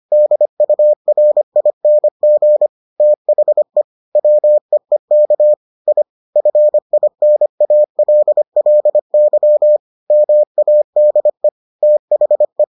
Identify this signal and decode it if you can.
Morse code, heard as a keyed tone, about 25 wpm, 600 Hz. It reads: DURING THE WEEK I FINALLY MADE THE